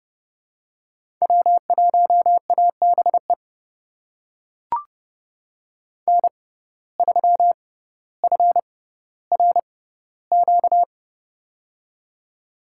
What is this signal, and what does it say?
Morse code, heard as a keyed tone, about 30 wpm, spaced as for 15 wpm, 700 Hz. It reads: W1ABE N3FRQ
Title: USA Call Signs - 6x Character Spacing - 30wpm